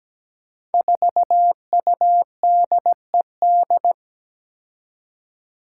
Morse code, heard as a keyed tone, about 17 wpm, 700 Hz. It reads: 4UDED
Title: Five Challenging Letter-Number Combinations 17wpm